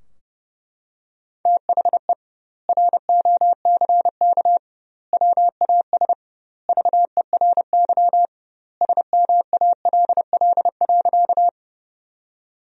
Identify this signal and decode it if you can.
Morse code, heard as a keyed tone, about 30 wpm, 700 Hz. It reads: THE ROCK WAS VERY SMALL.
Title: 2x Word-Spacing - Sentences from Top 400 Words - 30wpm